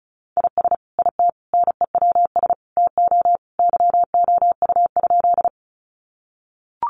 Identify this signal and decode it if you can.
Morse code, heard as a keyed tone, about 34 wpm, 700 Hz. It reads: IS IT NEWS TO YOU?
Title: Sentences from Top 800 Words 35wpm